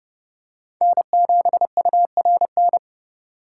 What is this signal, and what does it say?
Morse code, heard as a keyed tone, about 30 wpm, 700 Hz. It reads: N7URN
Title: USA Call Signs - 5 Characters - 6x Character Spacing - 30wpm